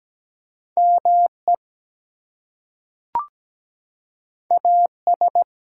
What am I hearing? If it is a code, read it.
Morse code, heard as a keyed tone, about 17 wpm, 700 Hz. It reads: ME AS
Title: Two Letter Words - 17wpm